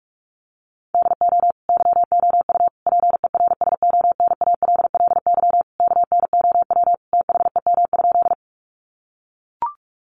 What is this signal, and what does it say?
Morse code, heard as a keyed tone, about 44 wpm, 700 Hz. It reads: DO YOU PERSONALLY KNOW THEM?
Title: Sentences from Top 1400 Words 45wpm